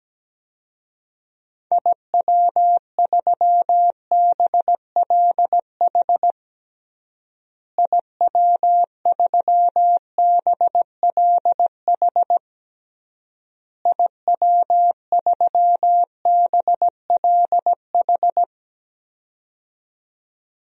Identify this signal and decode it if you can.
Morse code, heard as a keyed tone, about 17 wpm, 700 Hz. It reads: IW3BLH IW3BLH IW3BLH
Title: DXCC-100 Call Signs - 6 Characters - Repeated 3x - 17wpm